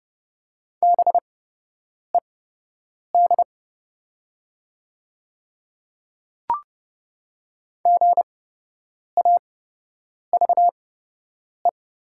Morse code, heard as a keyed tone, about 30 wpm, spaced as for 8 wpm, 700 Hz. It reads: BED GAVE
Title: 8x Character Spacing - Top 500 Words - 30wpm